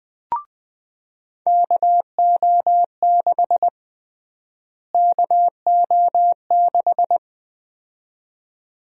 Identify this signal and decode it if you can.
Morse code, heard as a keyed tone, about 20 wpm, 700 Hz. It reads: KO6 KO6
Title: USA Call Sign Prefixes - Repeated 2x - 20wpm